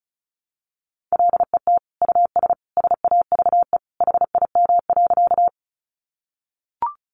Morse code, heard as a keyed tone, about 34 wpm, 700 Hz. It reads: LET US SAVE HIM.
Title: Sentences from Top 1100 Words 35wpm